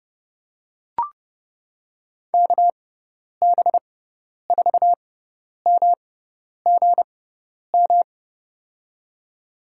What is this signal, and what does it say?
Morse code, heard as a keyed tone, about 30 wpm, spaced as for 10 wpm, 700 Hz. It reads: KB4MGM
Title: USA Call Signs - 6 Characters - 6x Character Spacing - 30wpm